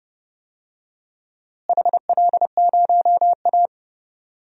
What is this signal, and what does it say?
Morse code, heard as a keyed tone, about 32 wpm, 700 Hz. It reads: HL0A
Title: DXCC-100 Call Signs - 4 Characters - Repeated 2x - 30wpm